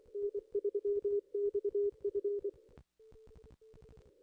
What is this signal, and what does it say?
Morse code, heard as a keyed tone, about 24 wpm, 410 Hz. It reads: N3XF
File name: contests / ss-cw-k5tr-op-n5rz -0100